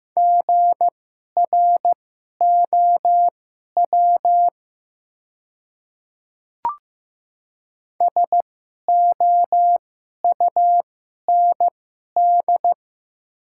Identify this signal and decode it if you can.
Morse code, heard as a keyed tone, about 15 wpm, spaced as for 11 wpm, 700 Hz. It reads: GROW SOUND
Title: 2x Character Spacing - Top 300 Words - 15wpm